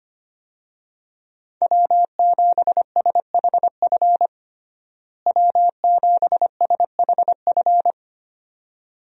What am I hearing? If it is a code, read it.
Morse code, heard as a keyed tone, about 25 wpm, 690 Hz. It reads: W7SHF W7SHF
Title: USA Call Signs - 5 Characters - Repeated 2x - 25wpm